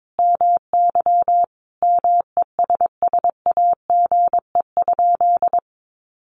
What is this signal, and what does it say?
Morse code, heard as a keyed tone, about 22 wpm, 700 Hz. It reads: MY MESSAGE?
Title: Sentences from Top 1100 Words 22wpm